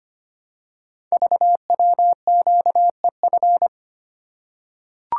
Morse code, heard as a keyed tone, about 25 wpm, 700 Hz. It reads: VWQEF
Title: Code Groups - 5 Letters - 25wpm